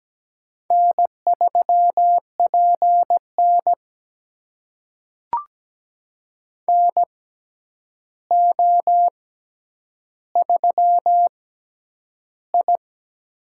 Morse code, heard as a keyed tone, about 17 wpm, spaced as for 9 wpm, 700 Hz. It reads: N3PN NO3I
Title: USA Call Signs - 4 Characters - 6x Character Spacing - 17wpm